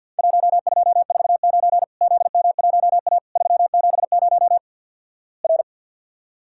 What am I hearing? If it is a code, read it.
Morse code, heard as a keyed tone, about 50 wpm, 700 Hz. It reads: 1249 ZM1A 370 R
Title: Worked All Europe DX Contest - QTC Exchange with Repeats - 50wpm